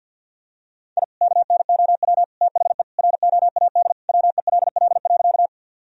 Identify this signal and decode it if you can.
Morse code, heard as a keyed tone, about 50 wpm, 700 Hz. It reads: I KNOW THE ROAD WELL.
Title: Sentences from Top 1000 Words - Review Part 2 - 50wpm